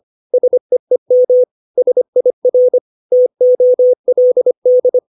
Morse code, heard as a keyed tone, about 25 wpm, 500 Hz. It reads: SEEM SIR TOLD
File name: 632 words / 632 Words 25WPM